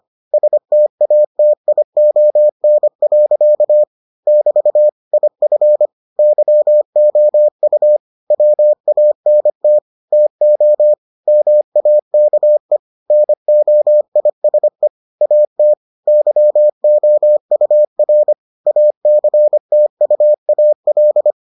Morse code, heard as a keyed tone, about 25 wpm, 600 Hz. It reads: STATION. <BT> IF YOU WANT TO MAKE NOISE AT YOUR ACTUAL